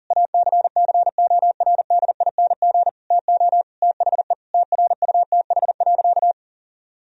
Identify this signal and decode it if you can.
Morse code, heard as a keyed tone, about 40 wpm, 700 Hz. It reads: ACCORDING TO THE TRUTH.
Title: Sentences from Top 1000 Words - Review Part 4 - 40wpm